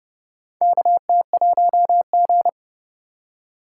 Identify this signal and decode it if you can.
Morse code, heard as a keyed tone, about 30 wpm, 700 Hz. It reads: KT1G
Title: USA Call Signs - 4 Characters - 4x Character Spacing - 30wpm